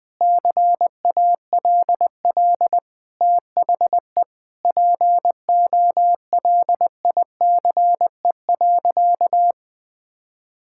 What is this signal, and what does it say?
Morse code, heard as a keyed tone, about 20 wpm, 700 Hz. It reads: CALL THE POLICE.
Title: Sentences with 7-Word Phrases - 20wpm